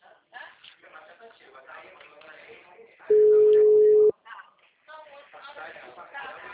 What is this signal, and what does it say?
Ringback tone (Europe, 425 Hz), with speech.